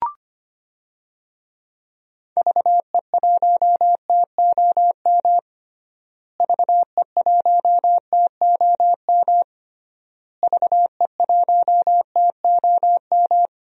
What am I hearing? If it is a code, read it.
Morse code, heard as a keyed tone, about 25 wpm, 700 Hz. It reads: VE1TOM VE1TOM VE1TOM
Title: DXCC-100 Call Signs - 6 Characters - Repeated 3x - 25wpm